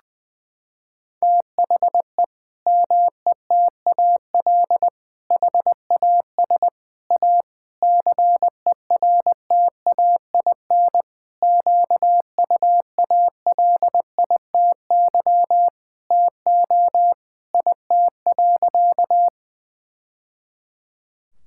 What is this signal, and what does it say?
Morse code, heard as a keyed tone, about 20 wpm, 700 Hz. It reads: THE METAL HAS A CERTAIN QUALITY TO IT.
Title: Sentences from Top 1000 Words - Review Part 6 - 20wpm